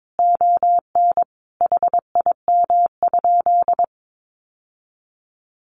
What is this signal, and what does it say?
Morse code, heard as a keyed tone, about 22 wpm, 700 Hz. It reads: ON HIM?
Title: Sentences from Top 1100 Words 22wpm